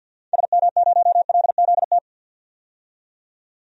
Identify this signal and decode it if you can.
Morse code, heard as a keyed tone, about 50 wpm, 700 Hz. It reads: SM0LZT
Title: DXCC-100 Call Signs - 6 Characters - 50wpm